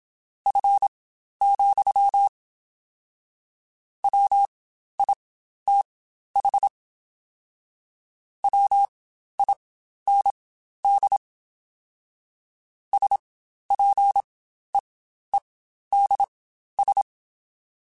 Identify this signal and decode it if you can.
Morse code, heard as a keyed tone, about 27 wpm, spaced as for 12 wpm, 800 Hz. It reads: F, WITH WIND SPEEDS
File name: Spaced Characters / Spaced Characters-mp3s / Weather4 21 sp @25